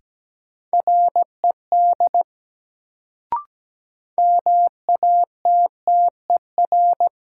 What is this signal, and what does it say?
Morse code, heard as a keyed tone, about 17 wpm, 700 Hz. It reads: RED MATTER
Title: Sentences from Top 700 Words - New Word Drill - 17wpm